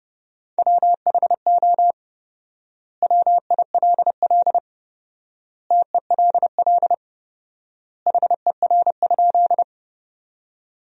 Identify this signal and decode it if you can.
Morse code, heard as a keyed tone, about 32 wpm, 700 Hz. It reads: WHO WILL TELL HER?
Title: 4x Word-Spacing - Sentences from Top 200 Words - 30wpm